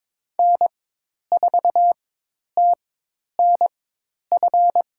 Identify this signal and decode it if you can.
Morse code, heard as a keyed tone, about 22 wpm, spaced as for 10 wpm, 700 Hz. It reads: N4TNF
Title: USA Call Signs - 5 Characters - 4x Character Spacing - 22wpm